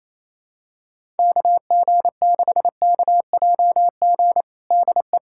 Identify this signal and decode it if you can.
Morse code, heard as a keyed tone, about 28 wpm, 700 Hz. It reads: KG6KJG DE
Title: USA Callsign DE Callsign - 6 Characters - 28wpm